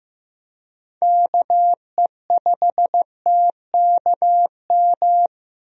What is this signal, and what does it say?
Morse code, heard as a keyed tone, about 15 wpm, 700 Hz. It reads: KE5TKM
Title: USA Call Signs - 6 Characters - 15wpm